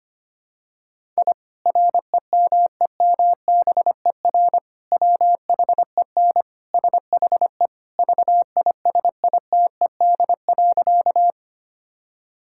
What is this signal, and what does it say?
Morse code, heard as a keyed tone, about 25 wpm, 700 Hz. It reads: I REMEMBER WHEN SHE VISITED.
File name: practice / 25wpm / Sentences with 2-Word Phrases - 25wpm